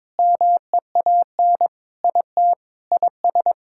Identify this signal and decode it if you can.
Morse code, heard as a keyed tone, about 22 wpm, 700 Hz. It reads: MEAN IT IS
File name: practice / 22wpm / Sentences with 4-Word Phrases - 22wpm